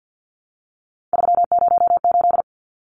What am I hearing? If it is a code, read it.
Morse code, heard as a keyed tone, about 50 wpm, 700 Hz. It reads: 308
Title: Numbers - 3-Digit 50wpm